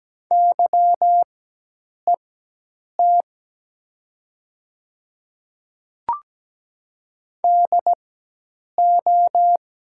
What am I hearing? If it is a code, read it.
Morse code, heard as a keyed tone, about 17 wpm, spaced as for 8 wpm, 700 Hz. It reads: YET DO
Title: 4x Character Spacing - Top 500 Words - 17wpm